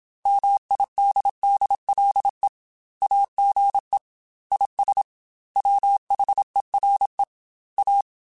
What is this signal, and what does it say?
Morse code, heard as a keyed tone, about 27 wpm, 800 Hz. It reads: MIDDLE AGE IS WHERE A
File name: Chuckles / Chuckles-mp3s / MiddleAge @25